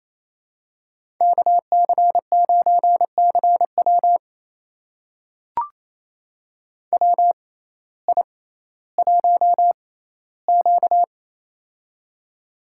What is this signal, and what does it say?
Morse code, heard as a keyed tone, about 28 wpm, spaced as for 15 wpm, 700 Hz. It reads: KC9CW WI1Q
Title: USA Call Signs - 6x Character Spacing - 28wpm